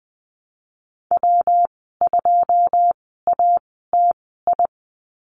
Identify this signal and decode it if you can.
Morse code, heard as a keyed tone, about 20 wpm, spaced as for 15 wpm, 700 Hz. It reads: W2ATI
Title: 128 Lesson 21g - Number 6 - Call Signs 20wpm